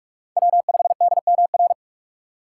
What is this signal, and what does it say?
Morse code, heard as a keyed tone, about 47 wpm, 690 Hz. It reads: W5DMR
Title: USA Call Signs - 5 Characters - 2x Character Spacing - 45wpm